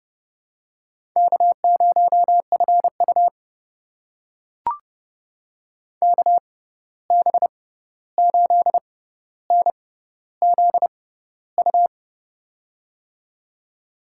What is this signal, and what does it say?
Morse code, heard as a keyed tone, about 30 wpm, spaced as for 14 wpm, 700 Hz. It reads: K0FU KB8NZU
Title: USA Call Signs - 6x Character Spacing - 30wpm